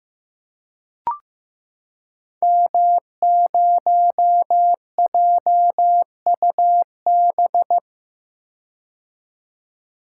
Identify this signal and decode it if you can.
Morse code, heard as a keyed tone, about 15 wpm, 700 Hz. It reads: M0JUB
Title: Code Groups - 5 Letter:Numbers - 15wpm